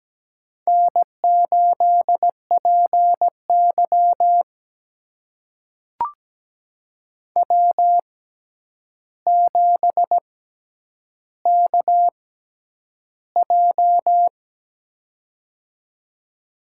Morse code, heard as a keyed tone, about 17 wpm, spaced as for 9 wpm, 700 Hz. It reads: N8PY W7KJ
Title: USA Call Signs - 4 Characters - 6x Character Spacing - 17wpm